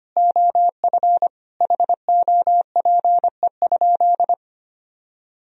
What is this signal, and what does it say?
Morse code, heard as a keyed tone, about 25 wpm, 700 Hz. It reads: OF HOPE?
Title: Sentences from Top 1000 Words - Review Part 3 - 25wpm